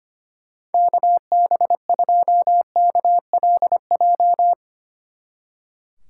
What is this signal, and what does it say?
Morse code, heard as a keyed tone, about 31 wpm, 700 Hz. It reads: KB2KLJ
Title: Speed Racing - USA 2x3 Call Signs 25wpm